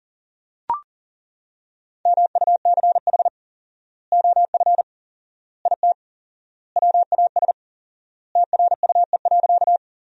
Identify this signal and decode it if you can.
Morse code, heard as a keyed tone, about 40 wpm, 700 Hz. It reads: MUCH OF IT WAS TRUE.
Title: 4x Word-Spacing - Sentences from Top 400 Words - 40wpm